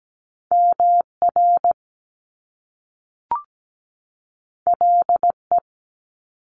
Morse code, heard as a keyed tone, about 17 wpm, 700 Hz. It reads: MR LE
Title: Top Two Letter Combinations in Words 17wpm